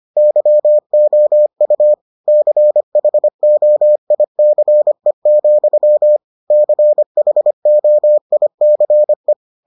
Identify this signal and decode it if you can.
Morse code, heard as a keyed tone, about 25 wpm, 600 Hz. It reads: YOU CHOICE, CHOICE